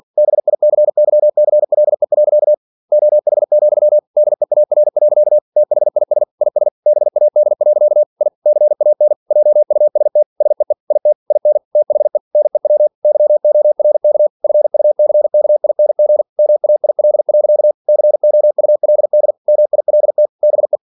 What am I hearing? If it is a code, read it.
Morse code, heard as a keyed tone, about 47 wpm, 600 Hz. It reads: BICYCLE. OH, DEAR. THIS IS BAD. I CAN JUST SEE IT IN THE NEW YORK FUCKING MAIL. COULD MILT BE
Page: Never 47wpm